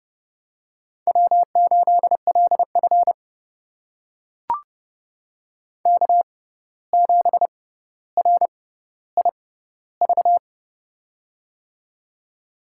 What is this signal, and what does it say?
Morse code, heard as a keyed tone, about 30 wpm, spaced as for 14 wpm, 700 Hz. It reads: W8LF K7RIV
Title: USA Call Signs - 6x Character Spacing - 30wpm